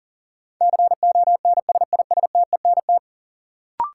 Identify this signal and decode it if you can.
Morse code, heard as a keyed tone, about 40 wpm, 700 Hz. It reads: CONSISTENT
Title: Sentences from Top 1500 Words - New Word Drill Alt - 40wpm